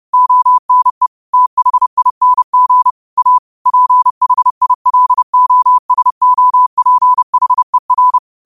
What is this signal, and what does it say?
Morse code, heard as a keyed tone, about 30 wpm, 1000 Hz. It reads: ONE THING A PHILOSOPHER
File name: qotdfiles / 30wpm